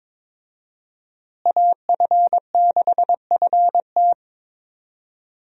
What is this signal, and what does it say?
Morse code, heard as a keyed tone, about 22 wpm, 700 Hz. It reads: AF6FT
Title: USA Call Signs - 5 Characters - Repeated 3x - 22wpm